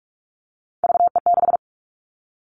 Morse code, heard as a keyed tone, about 44 wpm, 700 Hz. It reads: VE6
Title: All Cut Numbers - 3-Digit 45wpm